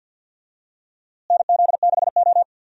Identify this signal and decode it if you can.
Morse code, heard as a keyed tone, about 51 wpm, 700 Hz. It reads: NZ6O
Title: USA Call Signs - 4 Characters - Repeated 3x - 50wpm